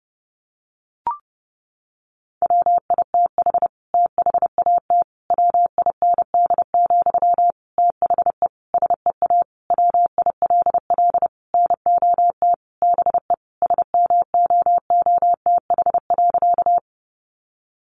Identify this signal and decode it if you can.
Morse code, heard as a keyed tone, about 30 wpm, 700 Hz. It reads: WITH THAT WIND, THE SEA WILL NOT BE SMOOTH.
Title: Sentences from Top 1400 Words 30wpm